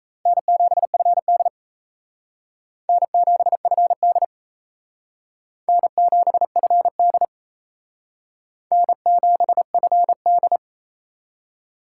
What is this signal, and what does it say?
Morse code, heard as a keyed tone, about 34 wpm, 700 Hz. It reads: N7FD N7FD N7FD N7FD
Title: Speed Racing - USA 1x2 Call Signs 28wpm